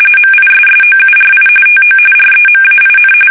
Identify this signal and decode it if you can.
RTTY, about 45.45 baud. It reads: MISSOURI TO REMAIN
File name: fsk850 / battleshipnj